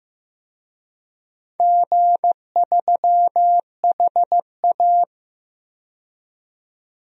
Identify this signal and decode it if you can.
Morse code, heard as a keyed tone, about 15 wpm, 700 Hz. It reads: G3HA
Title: DXCC-100 Call Signs - 4 Characters - 15wpm